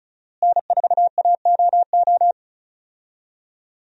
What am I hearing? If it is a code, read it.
Morse code, heard as a keyed tone, about 36 wpm, 700 Hz. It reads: N4AOO
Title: USA Call Signs - 5 Characters - Repeated 2x - 35wpm